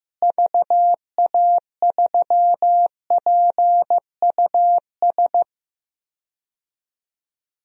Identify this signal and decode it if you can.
Morse code, heard as a keyed tone, about 15 wpm, 700 Hz. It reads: VA3PUS
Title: DXCC-100 Call Signs - 6 Characters - Repeated 2x - 15wpm